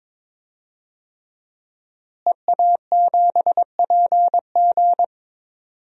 Morse code, heard as a keyed tone, about 22 wpm, 700 Hz. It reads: EA7PG
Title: DXCC-100 Call Signs - 5 Characters - Repeated 3x - 22wpm